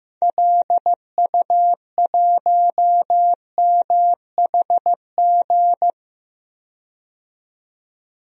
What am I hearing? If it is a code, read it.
Morse code, heard as a keyed tone, about 16 wpm, 700 Hz. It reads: LU1MHG
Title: DXCC-100 Call Signs - 6 Characters - 15wpm